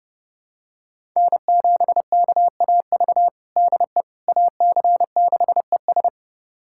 Morse code, heard as a keyed tone, about 30 wpm, 700 Hz. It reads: N7KAV DE AC6ES
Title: USA Callsign DE Callsign - 5 Characters - 30wpm